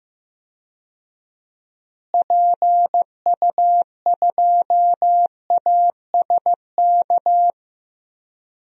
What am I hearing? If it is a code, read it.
Morse code, heard as a keyed tone, about 15 wpm, 700 Hz. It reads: PU2ASK
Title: DXCC-100 Call Signs - 6 Characters - Repeated 3x - 15wpm